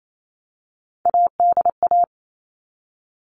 Morse code, heard as a keyed tone, about 28 wpm, 700 Hz. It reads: ADA
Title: All Cut Numbers - 3-Digit 28wpm